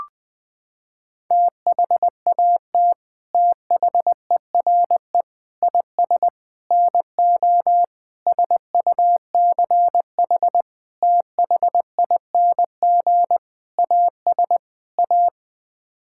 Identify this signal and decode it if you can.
Morse code, heard as a keyed tone, about 21 wpm, 700 Hz. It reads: THAT THERE IS NO SUCH THING AS A
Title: Sets of 8 Words - 20wpm